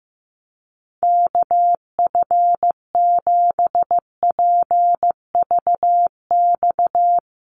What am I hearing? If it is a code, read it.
Morse code, heard as a keyed tone, about 15 wpm, 700 Hz. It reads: KF7PVX
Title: USA Call Signs Repeated 3x 15wpm